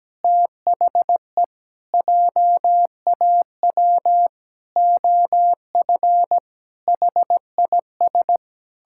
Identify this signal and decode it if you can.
Morse code, heard as a keyed tone, about 17 wpm, 700 Hz. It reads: THE JAW OF HIS
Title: Pangram Challenge v2 - 17wpm